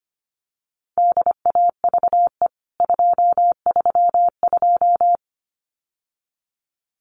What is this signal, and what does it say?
Morse code, heard as a keyed tone, about 25 wpm, 700 Hz. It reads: DAVE 232
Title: CWT Exchange 25wpm